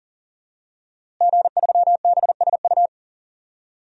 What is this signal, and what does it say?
Morse code, heard as a keyed tone, about 40 wpm, 700 Hz. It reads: G3BSU
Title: DXCC-100 Call Signs - 5 Characters - Repeated 3x - 40wpm